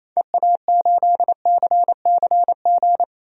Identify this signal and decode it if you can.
Morse code, heard as a keyed tone, about 28 wpm, 700 Hz. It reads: EA8CCG
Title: DXCC-100 Call Signs - 6 Characters - 28wpm